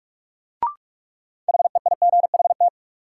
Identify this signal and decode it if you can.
Morse code, heard as a keyed tone, about 41 wpm, 700 Hz. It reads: HEIGHT
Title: Sentences from Top 1600 Words - New Word Drill Alt - 45wpm